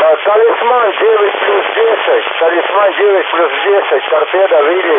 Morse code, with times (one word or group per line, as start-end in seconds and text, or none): none